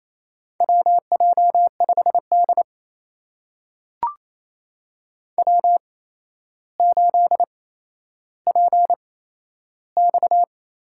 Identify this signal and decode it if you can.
Morse code, heard as a keyed tone, about 28 wpm, spaced as for 12 wpm, 700 Hz. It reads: WJ5D W8PX
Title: USA Call Signs - 4 Characters - 8x Character Spacing - 28wpm